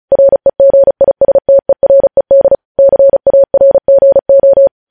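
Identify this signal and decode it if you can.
Morse code, heard as a keyed tone, about 36 wpm, 550 Hz. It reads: REGISTERED CARGO
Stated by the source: Morse news headlines sent at 35.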